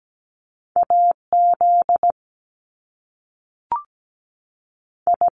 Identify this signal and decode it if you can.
Morse code, heard as a keyed tone, about 17 wpm, 700 Hz. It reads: AZ I
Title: US States 17wpm